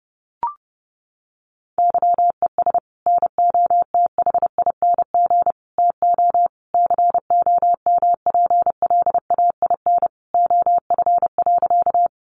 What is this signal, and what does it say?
Morse code, heard as a keyed tone, about 30 wpm, 700 Hz. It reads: YES NOTHING TO COMPLAIN OF.
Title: Sentences from Top 1300 Words 30wpm